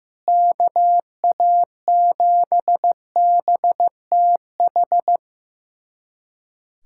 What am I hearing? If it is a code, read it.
Morse code, heard as a keyed tone, about 15 wpm, 700 Hz. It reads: KA7BTH
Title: USA Callsign DE Callsign 15wpm